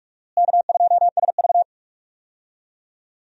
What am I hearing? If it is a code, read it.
Morse code, heard as a keyed tone, about 46 wpm, 700 Hz. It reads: K2SV
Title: USA Call Signs - 4 Characters - Repeated 3x - 45wpm